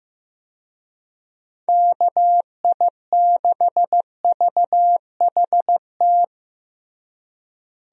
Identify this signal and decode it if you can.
Morse code, heard as a keyed tone, about 15 wpm, 700 Hz. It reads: KI6VHT
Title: USA Call Signs - 6 Characters - Repeated 3x - 15wpm